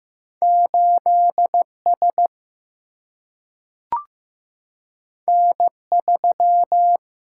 Morse code, heard as a keyed tone, about 15 wpm, 700 Hz. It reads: 8 S N 3
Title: Single-Letter Single-Number 15wpm